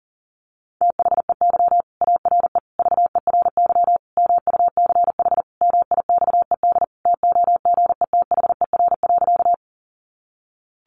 Morse code, heard as a keyed tone, about 40 wpm, 700 Hz. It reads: THEY ARE VERY MUCH MIXED TOGETHER.
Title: Sentences from Top 1500 Words 40wpm